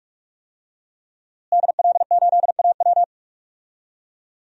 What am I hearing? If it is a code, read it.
Morse code, heard as a keyed tone, about 44 wpm, 700 Hz. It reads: DL8AW
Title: DXCC-100 Call Signs - 5 Characters - Repeated 2x - 45wpm